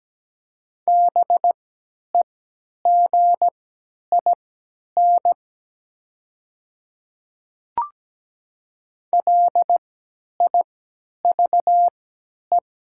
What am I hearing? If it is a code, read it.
Morse code, heard as a keyed tone, about 17 wpm, spaced as for 10 wpm, 700 Hz. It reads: BEGIN LIVE
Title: 3x Character Spacing - Top 300 Words - 17wpm